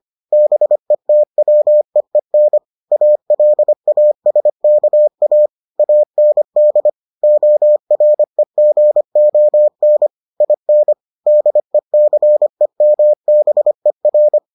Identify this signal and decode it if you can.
Morse code, heard as a keyed tone, about 25 wpm, 600 Hz. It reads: BETWEEN ALASKA AND OREGON IN DECEMBER